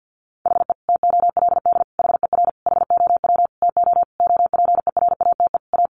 Morse code, heard as a keyed tone, about 50 wpm, 700 Hz. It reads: HE TOLD HER HOW TO OPERATE A
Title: Sentences from Top 1500 Words 50wpm